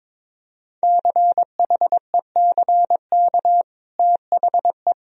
Morse code, heard as a keyed tone, about 22 wpm, 700 Hz. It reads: CHECK THE
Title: Sentences from Top 1000 Words - Review Part 3 - 22wpm